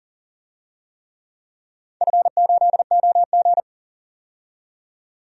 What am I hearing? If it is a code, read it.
Morse code, heard as a keyed tone, about 41 wpm, 700 Hz. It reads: F8OG
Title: DXCC-100 Call Signs - 4 Characters - 40wpm